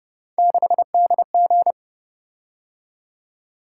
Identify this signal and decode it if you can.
Morse code, heard as a keyed tone, about 28 wpm, 700 Hz. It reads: 6DG